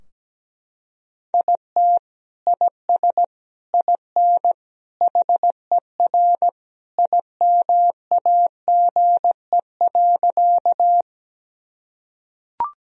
Morse code, heard as a keyed tone, about 17 wpm, 700 Hz. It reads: IT IS IN HER IMAGE.
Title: Sentences from Top 1000 Words - Review Part 4 - 17wpm